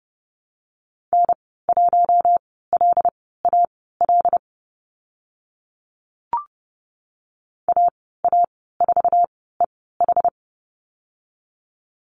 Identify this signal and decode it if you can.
Morse code, heard as a keyed tone, about 31 wpm, spaced as for 17 wpm, 690 Hz. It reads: N1LAL AA4EH
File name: Lesson 15g - Number 5 - Call Signs 30wpm